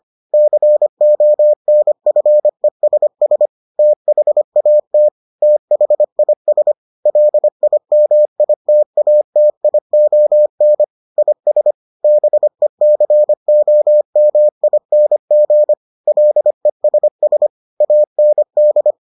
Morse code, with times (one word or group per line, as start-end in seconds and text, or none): 0.3-3.5 CONFESS
3.8-5.1 THAT
5.4-6.7 THIS
7.1-10.8 LIMITATION
11.2-11.7 IS
12.0-15.7 BECOMING
16.1-17.5 LESS
17.8-18.9 AND